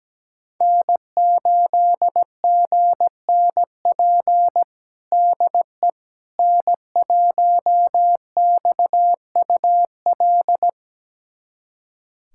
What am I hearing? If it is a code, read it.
Morse code, heard as a keyed tone, about 17 wpm, 700 Hz. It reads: N8GNP DE N1XUL
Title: USA Callsign DE Callsign - 5 Characters - 17wpm